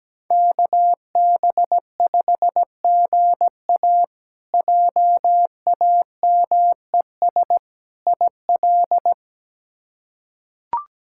Morse code, heard as a keyed tone, about 17 wpm, 700 Hz. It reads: KB5GA JAMES IL
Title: K1USN SST QSO Practice - 17wpm